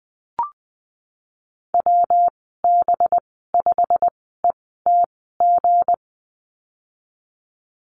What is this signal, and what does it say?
Morse code, heard as a keyed tone, about 19 wpm, spaced as for 14 wpm, 700 Hz. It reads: WB5ETG
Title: 193 Lesson 31g - Letter v - Call Signs 20wpm